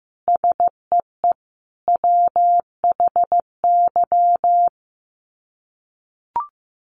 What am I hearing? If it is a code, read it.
Morse code, heard as a keyed tone, about 15 wpm, 700 Hz. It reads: SEE WHY
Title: Sets of 5 Words Encore 15wpm